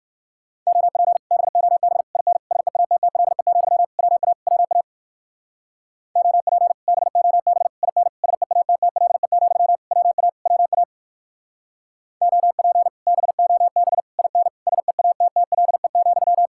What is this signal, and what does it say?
Morse code, heard as a keyed tone, about 56 wpm, 700 Hz. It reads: OP BOB IN SEATTLE, WA WA OP BOB IN SEATTLE, WA WA OP BOB IN SEATTLE,
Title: Speed-Racing Common QSO Phrases - 45wpm